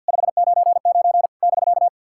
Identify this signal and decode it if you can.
Morse code, heard as a keyed tone, about 51 wpm, 700 Hz. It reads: 599 <BK>
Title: SOTA QSO Practice 50wpm